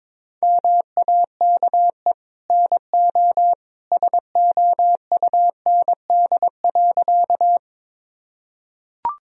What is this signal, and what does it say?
Morse code, heard as a keyed tone, about 21 wpm, 700 Hz. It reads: MAKE NO SOUND.
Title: Sentences from Top 1000 Words - Review Part 1 - 22wpm